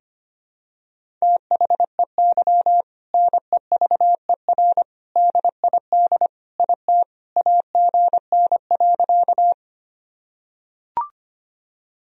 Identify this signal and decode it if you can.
Morse code, heard as a keyed tone, about 25 wpm, 700 Hz. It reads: THEY NEVER DID IT AGN.
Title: Sentences from Top 300 Words and Abbreviations 25wpm